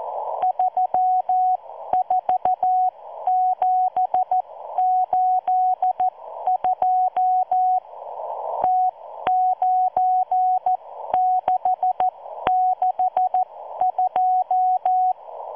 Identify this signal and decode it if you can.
Morse code, heard as a keyed tone, about 14 wpm, spaced as for 12 wpm, 740 Hz. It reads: 34782 T9662